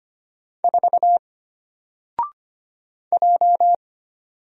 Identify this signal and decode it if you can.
Morse code, heard as a keyed tone, about 25 wpm, 700 Hz. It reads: 4 J
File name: Single-Letter-Number-25wpm